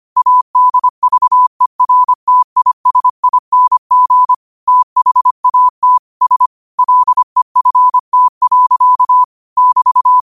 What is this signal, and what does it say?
Morse code, heard as a keyed tone, about 25 wpm, 1000 Hz. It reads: ADVERTISING THAT S LEFT. <BT>